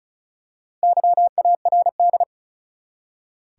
Morse code, heard as a keyed tone, about 34 wpm, 700 Hz. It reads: YARD
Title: Four Letter Words - 35wpm